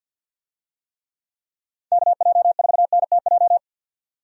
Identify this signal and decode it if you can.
Morse code, heard as a keyed tone, about 51 wpm, 700 Hz. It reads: KJ4NTJ
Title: USA Call Signs - 6 Characters - 50wpm